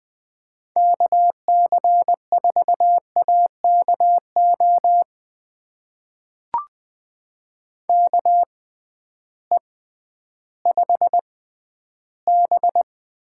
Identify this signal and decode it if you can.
Morse code, heard as a keyed tone, about 21 wpm, spaced as for 12 wpm, 700 Hz. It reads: KC4AKO KE5B
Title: USA Call Signs - 6 Characters - 6x Character Spacing - 20wpm